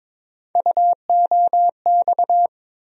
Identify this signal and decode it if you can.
Morse code, heard as a keyed tone, about 22 wpm, 700 Hz. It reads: UOX
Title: USA Call Sign Postfixes - 3 Letters 22wpm